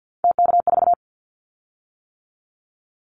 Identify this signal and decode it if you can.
Morse code, heard as a keyed tone, about 50 wpm, 690 Hz. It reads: TK4
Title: Callsign Prefixes for Top 100 DXCC - 50wpm